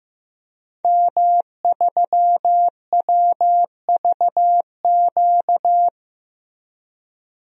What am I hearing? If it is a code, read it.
Morse code, heard as a keyed tone, about 15 wpm, 700 Hz. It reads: M3WVQ
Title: DXCC-100 Call Signs - 5 Characters - Repeated 2x - 15wpm